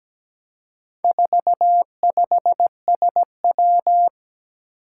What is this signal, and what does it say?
Morse code, heard as a keyed tone, about 17 wpm, 700 Hz. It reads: 45SW
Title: Four Challenging Letter-Number Combinations 17wpm